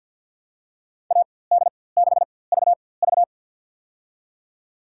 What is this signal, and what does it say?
Morse code, heard as a keyed tone, about 50 wpm, spaced as for 23 wpm, 700 Hz. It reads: AD6VV
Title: USA Call Signs - 5 Characters - 4x Character Spacing - 50wpm